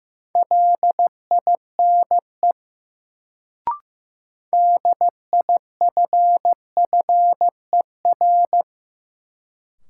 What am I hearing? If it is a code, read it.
Morse code, heard as a keyed tone, about 15 wpm, 700 Hz. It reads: LINE DIFFER
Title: Sentences from Top 1000 Words - Review Part 1 - New Word Drill - 15wpm